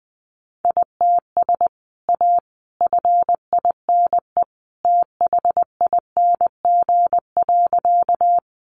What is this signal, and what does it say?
Morse code, heard as a keyed tone, about 20 wpm, 700 Hz. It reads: ITS A FINE THING.
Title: Sentences from Top 600 Words 20wpm